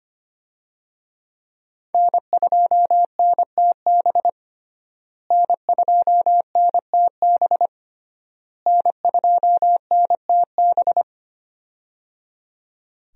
Morse code, heard as a keyed tone, about 25 wpm, 700 Hz. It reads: N2NTB N2NTB N2NTB
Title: USA Call Signs - 5 Characters - Repeated 3x - 25wpm